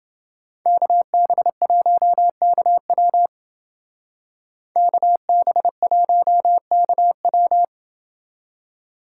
Morse code, heard as a keyed tone, about 29 wpm, 700 Hz. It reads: KB1KW KB1KW